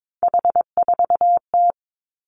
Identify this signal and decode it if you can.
Morse code, heard as a keyed tone, about 22 wpm, 700 Hz. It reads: H4T
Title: Three Challenging Letter-Number Combinations 22wpm